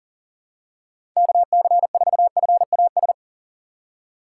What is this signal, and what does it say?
Morse code, heard as a keyed tone, about 40 wpm, 700 Hz. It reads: KC4FAS
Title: USA Call Signs - 4x Character Spacing - 40wpm